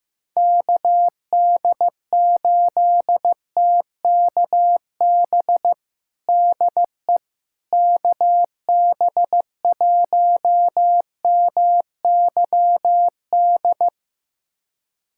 Morse code, heard as a keyed tone, about 15 wpm, 700 Hz. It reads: KD8TKB DE KB1MYD
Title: USA Callsign DE Callsign - 6 Characters - 15wpm